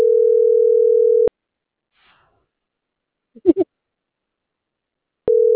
Ringback tone (North America), 440 + 480 Hz.